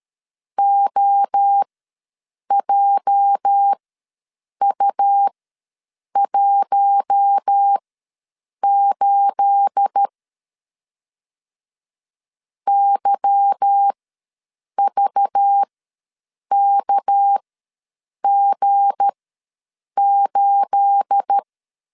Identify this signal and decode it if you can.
Morse code, heard as a keyed tone, about 13 wpm, spaced as for 7 wpm, 800 Hz. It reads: OJU18 YVKG8
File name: LEZIONI / 9 28 ws 12wpm 3354